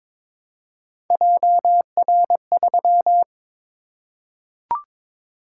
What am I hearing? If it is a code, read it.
Morse code, heard as a keyed tone, about 23 wpm, 700 Hz. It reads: JR3
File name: practice / 22wpm / Three Letter-Number Combinations - 22wpm